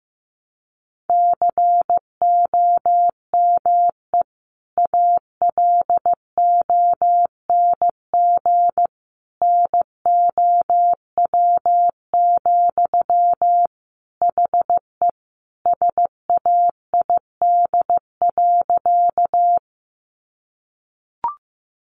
Morse code, heard as a keyed tone, about 15 wpm, 700 Hz. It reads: COME ALONG NOW, HE SAID.
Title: Sentences from Top 700 Words 15wpm